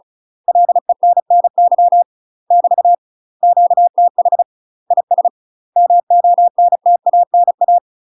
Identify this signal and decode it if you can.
Morse code, heard as a keyed tone, about 36 wpm, 700 Hz. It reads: LENNY <BT> QTH IS MONTANA
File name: UK MASTER QSOs TRAINING / 35WPM / UK MASTER QSOs 0026